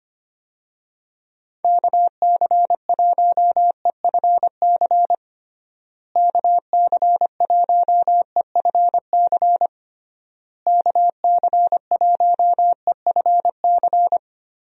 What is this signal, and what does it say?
Morse code, heard as a keyed tone, about 25 wpm, 700 Hz. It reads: KC1EFC KC1EFC KC1EFC
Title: USA Call Signs - 6 Characters - Repeated 3x - 25wpm